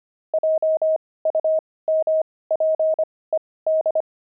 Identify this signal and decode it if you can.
Morse code, heard as a keyed tone, about 25 wpm, spaced as for 18 wpm, 630 Hz. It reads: JUMPED
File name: pangram-fs-18-cs-25-barking